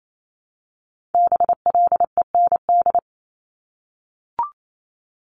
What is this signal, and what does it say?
Morse code, heard as a keyed tone, about 28 wpm, 700 Hz. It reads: BLEND
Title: Multi-letter Phonemes Repeated 3x 28wpm